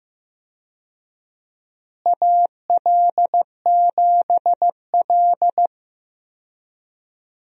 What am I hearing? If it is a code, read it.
Morse code, heard as a keyed tone, about 15 wpm, 700 Hz. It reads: AL7L
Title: USA Call Signs - 4 Characters - Repeated 3x - 15wpm